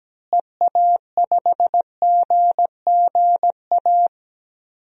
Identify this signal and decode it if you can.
Morse code, heard as a keyed tone, about 17 wpm, 700 Hz. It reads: EA5GGA
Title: DXCC-100 Call Signs - 6 Characters - Repeated 3x - 17wpm